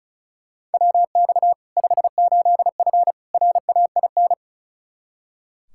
Morse code, heard as a keyed tone, about 36 wpm, 700 Hz. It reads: WX 58F RAIN
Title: Practice QSO - Exchange 2 - Info - 35wpm